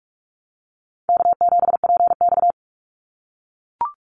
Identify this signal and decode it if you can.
Morse code, heard as a keyed tone, about 44 wpm, 700 Hz. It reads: K7PX
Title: NA SOTA Activator Callsigns 45wpm